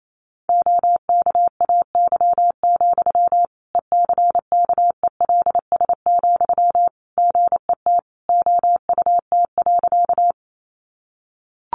Morse code, heard as a keyed tone, about 28 wpm, 700 Hz. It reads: OKAY, ECKELS, GET OUT.
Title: A Sound Of Thunder 28wpm